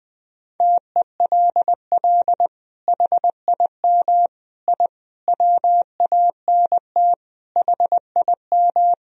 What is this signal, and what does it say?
Morse code, heard as a keyed tone, about 20 wpm, 700 Hz. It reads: TELL HIM I WANT HIM
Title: Sentences from Top 1000 Words - Review Part 2 - 20wpm